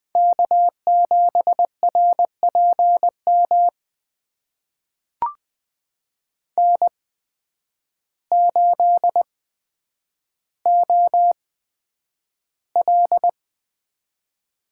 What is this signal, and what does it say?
Morse code, heard as a keyed tone, about 20 wpm, spaced as for 9 wpm, 700 Hz. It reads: K7RPM N8OL
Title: USA Call Signs - 5 Characters - 8x Character Spacing - 20wpm